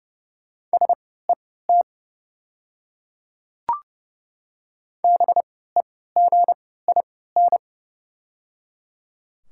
Morse code, heard as a keyed tone, about 30 wpm, spaced as for 17 wpm, 690 Hz. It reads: SET BEGIN
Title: 3x Character Spacing - Top 300 Words - 30wpm